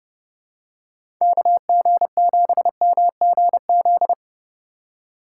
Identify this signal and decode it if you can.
Morse code, heard as a keyed tone, about 30 wpm, 700 Hz. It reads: KG7MGZ
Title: USA Call Signs - 6 Characters - 8x Character Spacing - 30wpm